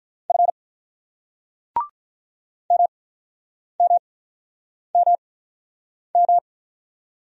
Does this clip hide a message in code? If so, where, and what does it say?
Yes: Morse code from 0.3 to 6.4 s, heard as a keyed tone, about 47 wpm, spaced as for 9 wpm, 700 Hz. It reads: F MMMM